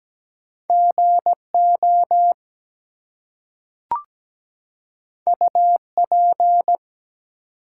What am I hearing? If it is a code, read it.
Morse code, heard as a keyed tone, about 17 wpm, 700 Hz. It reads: GO UP
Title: Two Letter Words - 17wpm